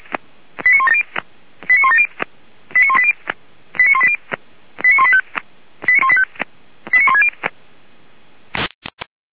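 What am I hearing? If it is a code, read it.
Five-tone selective call (ZVEI), heard as quick runs of five tones, tones of about 70 ms. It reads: 89170 89170 89180 89180 89185 89160
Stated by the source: Five-tone ZVEI selective call 89133, received by SDR, fire brigade Orlamünde (feuerwehr-orlamuende).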